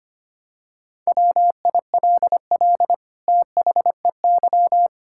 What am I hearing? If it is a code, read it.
Morse code, heard as a keyed tone, about 25 wpm, 700 Hz. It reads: WILL THEY
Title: Sentences from Top 1000 Words - Review Part 6 - 25wpm